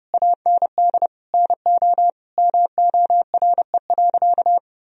Morse code, heard as a keyed tone, about 32 wpm, 700 Hz. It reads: AND NO MORE.
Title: Sentences from Top 1000 Words - Review Part 2 - 30wpm